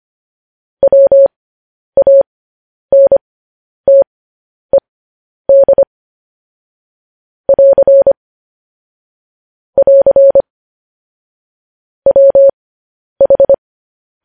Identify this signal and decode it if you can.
Morse code, heard as a keyed tone, about 25 wpm, spaced as for 10 wpm, 550 Hz. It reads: WANTED <AR> <AR> WH